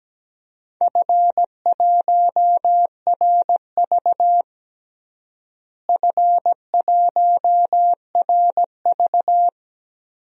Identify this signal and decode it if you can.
Morse code, heard as a keyed tone, about 17 wpm, 700 Hz. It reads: F1RV F1RV
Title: DXCC-100 Call Signs - 4 Characters - Repeated 3x - 17wpm